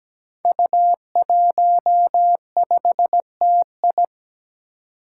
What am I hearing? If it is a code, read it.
Morse code, heard as a keyed tone, about 17 wpm, 700 Hz. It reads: U15TI
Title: Five Challenging Letter-Number Combinations 17wpm